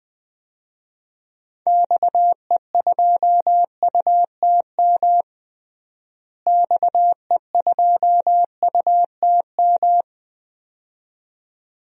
Morse code, heard as a keyed tone, about 20 wpm, 700 Hz. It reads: XE2UTM XE2UTM
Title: DXCC-100 Call Signs - 6 Characters - Repeated 2x - 20wpm